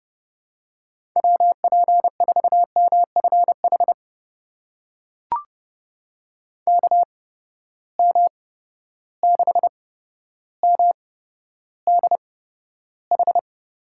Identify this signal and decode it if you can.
Morse code, heard as a keyed tone, about 30 wpm, spaced as for 13 wpm, 700 Hz. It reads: WP4MFH KM6MDH
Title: USA Call Signs - 6 Characters - 8x Character Spacing - 30wpm